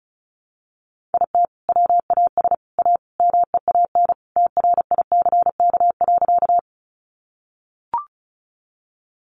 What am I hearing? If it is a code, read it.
Morse code, heard as a keyed tone, about 34 wpm, 700 Hz. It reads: IT WAS A MEAN TRICK.
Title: Sentences from Top 1500 Words 35wpm